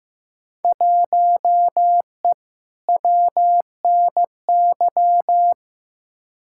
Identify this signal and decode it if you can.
Morse code, heard as a keyed tone, about 15 wpm, 700 Hz. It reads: 1E WNY
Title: Field Day Exchanges - 15wpm